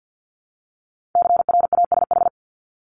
Morse code, heard as a keyed tone, about 50 wpm, 700 Hz. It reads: CRASH
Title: Multi-letter Phonemes 50wpm